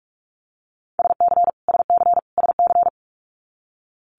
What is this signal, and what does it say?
Morse code, heard as a keyed tone, about 46 wpm, 700 Hz. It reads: SC SC SC
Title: Multi-letter Phonemes Repeated 3x 45wpm